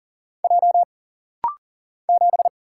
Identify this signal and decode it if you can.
Morse code, heard as a keyed tone, about 40 wpm, 700 Hz. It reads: J 7
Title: Single Letter-Number - Rapid-Fire 40wpm